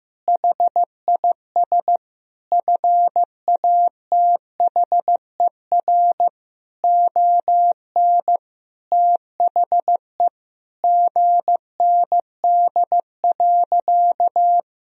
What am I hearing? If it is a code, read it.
Morse code, heard as a keyed tone, about 15 wpm, 700 Hz. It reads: HIS FATHER ON THE GND.
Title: Sentences from Top 500 Words and Abbreviations 15wpm